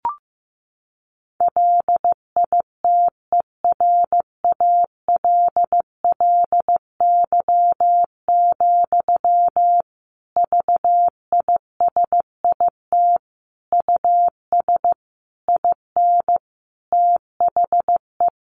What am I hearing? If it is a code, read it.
Morse code, heard as a keyed tone, about 15 wpm, 700 Hz. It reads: LITERALLY, VISIT US IN THE
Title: Sentences from Top 1400 Words 15wpm